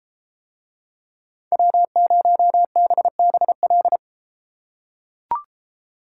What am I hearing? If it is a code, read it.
Morse code, heard as a keyed tone, about 33 wpm, 700 Hz. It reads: W0BBL